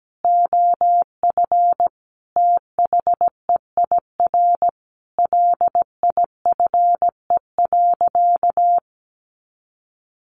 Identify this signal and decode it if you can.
Morse code, heard as a keyed tone, about 17 wpm, 700 Hz. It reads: OF THEIR LIFE.
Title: Sentences from Top 1200 Words 17wpm